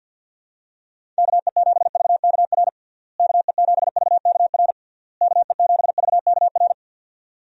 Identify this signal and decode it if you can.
Morse code, heard as a keyed tone, about 52 wpm, 700 Hz. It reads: KE7VKR KE7VKR KE7VKR
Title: USA Call Signs - 6 Characters - Repeated 3x - 50wpm